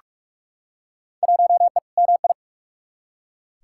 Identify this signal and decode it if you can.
Morse code, heard as a keyed tone, about 45 wpm, 700 Hz. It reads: 1E MI
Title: Field Day Exchanges - 45wpm